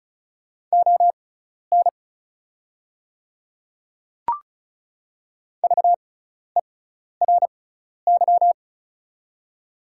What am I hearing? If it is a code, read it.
Morse code, heard as a keyed tone, about 36 wpm, spaced as for 12 wpm, 690 Hz. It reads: ON VERY
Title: 6x Character Spacing - Top 200 Words - 35wpm